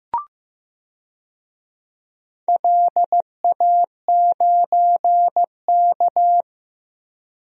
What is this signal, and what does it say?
Morse code, heard as a keyed tone, about 15 wpm, 700 Hz. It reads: LA9K
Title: DXCC-100 Call Signs - 4 Characters - Repeated 2x - 15wpm